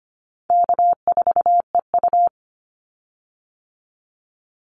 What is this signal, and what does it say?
Morse code, heard as a keyed tone, about 25 wpm, 700 Hz. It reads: K4EU
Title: Call Signs Repeated 3x - 25wpm